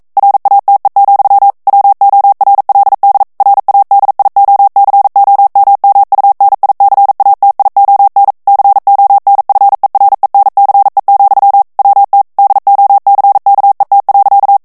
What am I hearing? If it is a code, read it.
Morse code, heard as a keyed tone, about 42 wpm, 800 Hz. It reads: RATE, WORLD RADIOCOMMUNICATION CONFERENCE, WT DOCKET.
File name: 40 mots minute